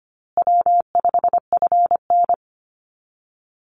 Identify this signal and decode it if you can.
Morse code, heard as a keyed tone, about 25 wpm, 700 Hz. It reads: W5FN
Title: Call Signs Repeated 2x - 25wpm